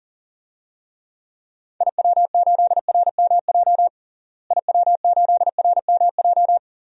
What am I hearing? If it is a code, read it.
Morse code, heard as a keyed tone, about 40 wpm, 700 Hz. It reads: IW8RMJ IW8RMJ
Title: DXCC-100 Call Signs - 6 Characters - Repeated 2x - 40wpm